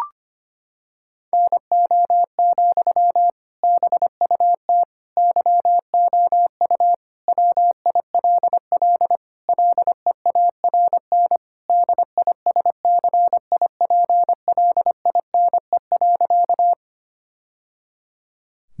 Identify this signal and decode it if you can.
Morse code, heard as a keyed tone, about 25 wpm, 700 Hz. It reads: NO, BUT YOU WILL LEARN DISCIPLINE.
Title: Sentences from Top 1000 Words - Review Part 6 - 25wpm